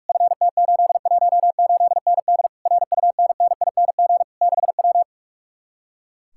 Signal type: Morse code

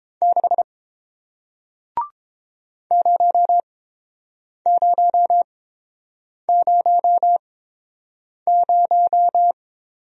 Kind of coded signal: Morse code

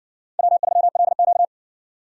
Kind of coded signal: Morse code

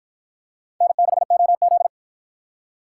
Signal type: Morse code

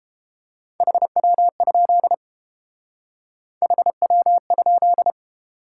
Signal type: Morse code

